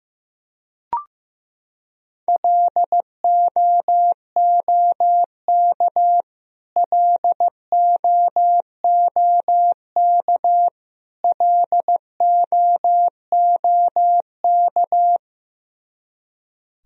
Morse code, heard as a keyed tone, about 16 wpm, 700 Hz. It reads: LOOK LOOK LOOK
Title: Top 100 Words Repeated 15wpm